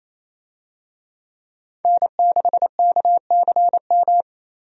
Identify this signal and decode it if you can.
Morse code, heard as a keyed tone, about 28 wpm, 700 Hz. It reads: N6KCM
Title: USA Call Signs - 5 Characters - Repeated 3x - 28wpm